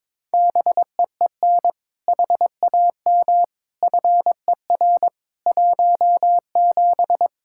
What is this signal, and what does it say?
Morse code, heard as a keyed tone, about 21 wpm, 700 Hz. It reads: BEEN HAM FER 17
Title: K1USN SST QSO Practice 22wpm